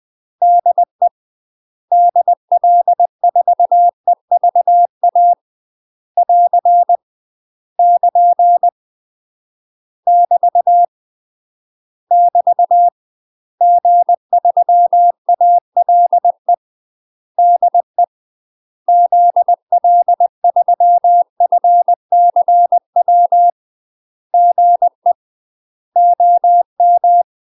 Morse code, heard as a keyed tone, about 20 wpm, 700 Hz. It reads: DE DL4EVA <AR> <KN> <BT> <BT> G3ALE DE ZL3FCW GE OM
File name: UK MASTER QSOs TRAINING / 20WPM / UK MASTER QSOs 0029